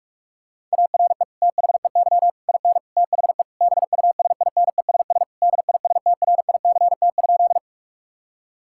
Morse code, heard as a keyed tone, about 46 wpm, 700 Hz. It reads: ARE THEY IN THE BUSINESS DISTRICT?
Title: Sentences from Top 1600 Words 45wpm